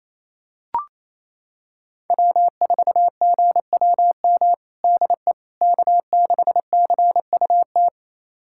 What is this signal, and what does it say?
Morse code, heard as a keyed tone, about 27 wpm, 700 Hz. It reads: W4GWM DE K6CUT
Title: USA Callsign DE Callsign - 5 Characters - 28wpm